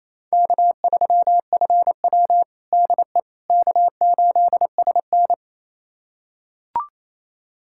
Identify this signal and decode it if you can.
Morse code, heard as a keyed tone, about 28 wpm, 700 Hz. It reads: K3FW DE K8SN
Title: USA Callsign DE Callsign - 4 Characters - 28wpm